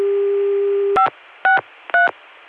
DTMF, heard as telephone touch tones, tones of about 130 ms. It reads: 563